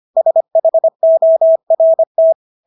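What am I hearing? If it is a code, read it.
Morse code, heard as a keyed tone, about 25 wpm, 650 Hz. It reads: SHORT
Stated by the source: Morse practice of English words at 25 wpm from CW med Gnister.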